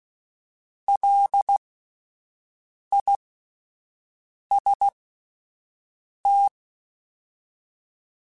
Morse code, heard as a keyed tone, about 16 wpm, spaced as for 5 wpm, 800 Hz. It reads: LIST
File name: Alphabetical / Alphabetical mp3s / Ls @5